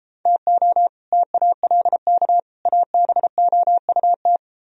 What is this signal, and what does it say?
Morse code, heard as a keyed tone, about 32 wpm, 700 Hz. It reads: TO TALK ABOUT